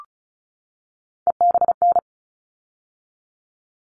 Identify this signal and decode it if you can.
Morse code, heard as a keyed tone, about 34 wpm, 700 Hz. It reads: EBN
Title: All Cut Numbers - 3-Digit 35wpm